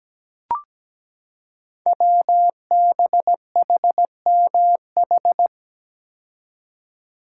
Morse code, heard as a keyed tone, about 17 wpm, 700 Hz. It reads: WBHMH
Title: Code Groups - 5 Letter:Numbers - 17wpm